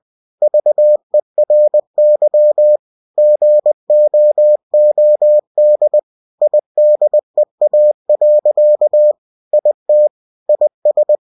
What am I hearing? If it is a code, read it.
Morse code, heard as a keyed tone, about 20 wpm, 600 Hz. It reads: VERY GOOD IDEA. IT IS
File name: cw-books / winnie-the-pooh / winnie-20wpm-page-0049